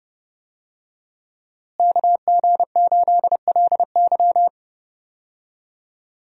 Morse code, heard as a keyed tone, about 30 wpm, 700 Hz. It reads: KG8LY
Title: USA Call Signs - 5 Characters - Repeated 3x - 30wpm